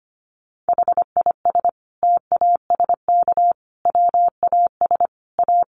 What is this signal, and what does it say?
Morse code, heard as a keyed tone, about 25 wpm, 700 Hz. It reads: HIS TASK WAS A
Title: Sentences from Top 1200 Words 25wpm